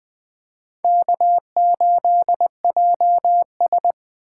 Morse code, heard as a keyed tone, about 20 wpm, 700 Hz. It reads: K8JS
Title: Speed Racing - USA 1x2 Call Signs 17wpm